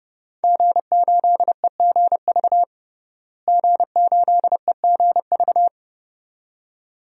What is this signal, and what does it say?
Morse code, heard as a keyed tone, about 30 wpm, 700 Hz. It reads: G8EGV G8EGV
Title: DXCC-100 Call Signs - 5 Characters - Repeated 2x - 30wpm